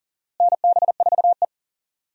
Morse code, heard as a keyed tone, about 40 wpm, 700 Hz. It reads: NB4E
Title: USA Call Signs - 4 Characters - Repeated 2x - 40wpm